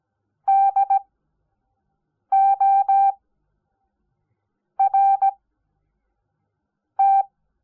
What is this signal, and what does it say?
Morse code, heard as a keyed tone, about 17 wpm, spaced as for 5 wpm, 780 Hz. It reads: DORT
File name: Onlinelehrgang / morsen / DJ4UF Morsen05